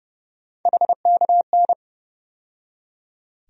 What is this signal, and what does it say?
Morse code, heard as a keyed tone, about 30 wpm, 700 Hz. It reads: HKN
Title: USA Call Sign Postfixes - 3 Letters 30wpm